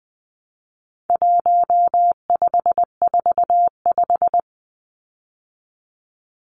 Morse code, heard as a keyed tone, about 20 wpm, 700 Hz. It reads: 1545
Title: Numbers - 4-Digit 20wpm